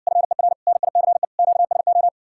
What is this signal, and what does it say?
Morse code, heard as a keyed tone, about 60 wpm, 700 Hz. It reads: FER NICE QSO